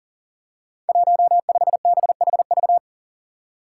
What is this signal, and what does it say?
Morse code, heard as a keyed tone, about 40 wpm, 700 Hz. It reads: 15BHV
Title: Five Challenging Letter-Number Combinations 40wpm